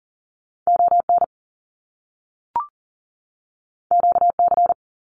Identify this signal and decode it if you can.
Morse code, heard as a keyed tone, about 40 wpm, 700 Hz. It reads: ON QC
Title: Canadian Provinces and Territories 40wpm